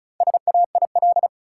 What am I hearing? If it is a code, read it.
Morse code, heard as a keyed tone, about 35 wpm, 700 Hz. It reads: SAIL
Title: Four Letter Words - 35wpm